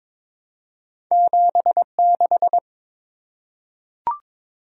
Morse code, heard as a keyed tone, about 22 wpm, 700 Hz. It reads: 76
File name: practice / 22wpm / Two Letter-Number Combinations - 22wpm